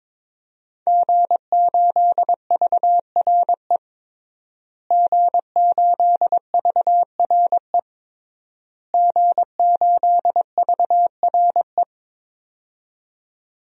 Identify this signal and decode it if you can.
Morse code, heard as a keyed tone, about 21 wpm, 700 Hz. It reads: G8VRE G8VRE G8VRE
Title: DXCC-100 Call Signs - 5 Characters - Repeated 3x - 22wpm